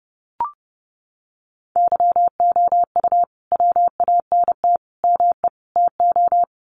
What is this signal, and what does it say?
Morse code, heard as a keyed tone, about 30 wpm, 700 Hz. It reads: YOU WANT ME TO
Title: Sets of 4 Words 30wpm